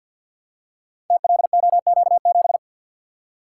Morse code, heard as a keyed tone, about 51 wpm, 700 Hz. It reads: TLOQ7
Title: Code Groups - 5 Letter:Numbers - 50wpm